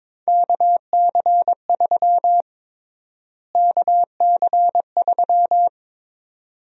Morse code, heard as a keyed tone, about 22 wpm, 700 Hz. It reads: KC3 KC3
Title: USA Call Sign Prefixes - Repeated 2x - 22wpm